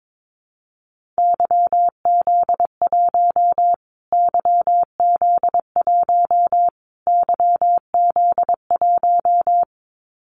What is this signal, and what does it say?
Morse code, heard as a keyed tone, about 22 wpm, 700 Hz. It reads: YZ1 YZ1 YZ1
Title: Callsign Prefixes for Top 25 DXCC - Repeated 3x - 22wpm